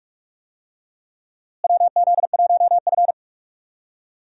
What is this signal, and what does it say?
Morse code, heard as a keyed tone, about 46 wpm, 700 Hz. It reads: WZ1F
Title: USA Call Signs - 4 Characters - Repeated 3x - 45wpm